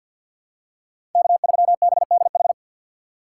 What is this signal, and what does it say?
Morse code, heard as a keyed tone, about 53 wpm, 700 Hz. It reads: K3BDH
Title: USA Call Signs - 5 Characters - 6x Character Spacing - 50wpm